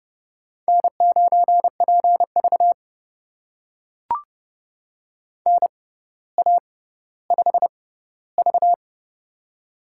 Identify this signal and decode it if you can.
Morse code, heard as a keyed tone, about 30 wpm, spaced as for 15 wpm, 700 Hz. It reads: N9PV NA5V
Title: USA Call Signs - 4 Characters - 6x Character Spacing - 30wpm